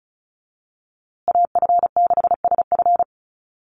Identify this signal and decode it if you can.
Morse code, heard as a keyed tone, about 34 wpm, 700 Hz. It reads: AF6SF
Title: USA Call Signs Repeated 3x 35wpm